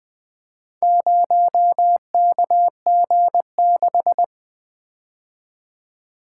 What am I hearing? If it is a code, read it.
Morse code, heard as a keyed tone, about 20 wpm, 700 Hz. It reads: 0KG6
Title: Four Letter-Number Combinations - 20wpm